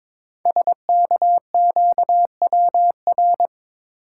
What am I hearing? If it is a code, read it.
Morse code, heard as a keyed tone, about 22 wpm, 700 Hz. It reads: SKQWR
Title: Code Groups - 5 Letter:Numbers - 22wpm